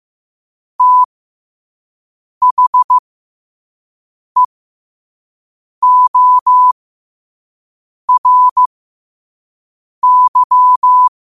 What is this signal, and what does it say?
Morse code, heard as a keyed tone, about 15 wpm, spaced as for 5 wpm, 1000 Hz. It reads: THEORY
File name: qotdfiles / 5wpm